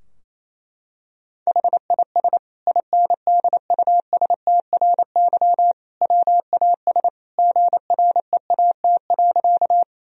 Morse code, heard as a keyed tone, about 28 wpm, 700 Hz. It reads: HIS INDUSTRY WAS GREAT.
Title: Sentences from Top 1000 Words - Review Part 4 - 28wpm